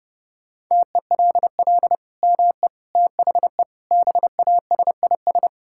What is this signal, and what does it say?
Morse code, heard as a keyed tone, about 30 wpm, 700 Hz. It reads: TELL ME THE BASIS
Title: Sentences from Top 1000 Words - Review Part 5 - 30wpm